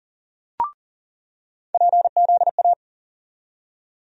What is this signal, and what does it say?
Morse code, heard as a keyed tone, about 40 wpm, 700 Hz. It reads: PZA
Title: USA Call Sign Postfixes - 3 Letters 40wpm